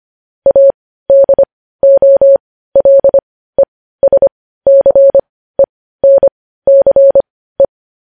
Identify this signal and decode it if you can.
Morse code, heard as a keyed tone, about 25 wpm, spaced as for 15 wpm, 550 Hz. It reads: ADOLESCENCE
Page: News Headlines in Morse Code at 15 WPM